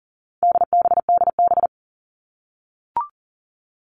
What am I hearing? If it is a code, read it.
Morse code, heard as a keyed tone, about 40 wpm, 700 Hz. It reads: DBDB
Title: All Cut Numbers - 4-Digit 40wpm